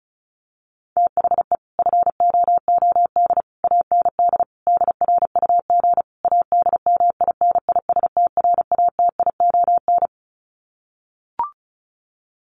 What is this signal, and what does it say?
Morse code, heard as a keyed tone, about 34 wpm, 700 Hz. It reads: THE FOOD AND DRUG ADMINISTRATION
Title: Sets of 5 Words 35wpm